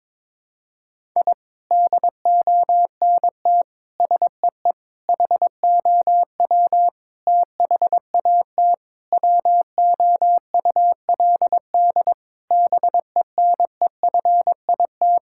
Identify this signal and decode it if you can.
Morse code, heard as a keyed tone, about 22 wpm, 700 Hz. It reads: I DONT SEE HOW THAT WOULD BENEFIT
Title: Sentences from Top 1000 Words - Review Part 6 - 22wpm